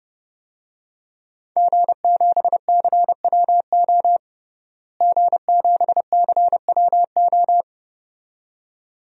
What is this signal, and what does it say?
Morse code, heard as a keyed tone, about 30 wpm, 700 Hz. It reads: G7CWO G7CWO
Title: DXCC-100 Call Signs - 5 Characters - Repeated 2x - 30wpm